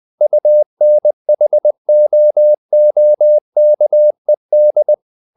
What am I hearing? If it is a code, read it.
Morse code, heard as a keyed tone, about 20 wpm, 600 Hz. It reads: UNHOOKED
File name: the old man and the sea part 2 20wpm